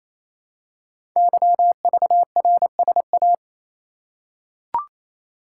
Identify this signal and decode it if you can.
Morse code, heard as a keyed tone, about 27 wpm, 700 Hz. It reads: YVRSA
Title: Code Groups - 5 Letters - 28wpm